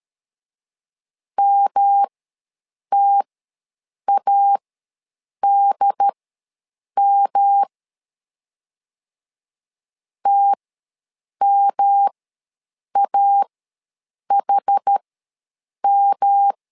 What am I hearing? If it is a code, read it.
Morse code, heard as a keyed tone, about 13 wpm, spaced as for 7 wpm, 800 Hz. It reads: MTADM TMAHM
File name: LEZIONI / 3 MWRO / 3 M ws 12wpm 1745